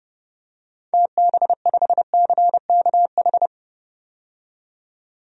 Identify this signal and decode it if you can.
Morse code, heard as a keyed tone, about 30 wpm, 700 Hz. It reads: TB5CKH
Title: DXCC-100 Call Signs - 6 Characters - 30wpm